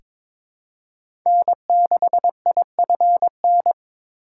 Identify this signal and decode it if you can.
Morse code, heard as a keyed tone, about 22 wpm, 690 Hz. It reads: N6IFN
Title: USA Call Signs - 5 Characters - 2x Character Spacing - 22wpm